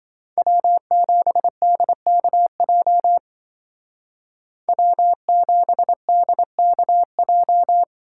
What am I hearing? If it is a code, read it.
Morse code, heard as a keyed tone, about 25 wpm, 700 Hz. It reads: W7DKJ W7DKJ
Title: Speed Racing - USA 1x3 Call Signs 20wpm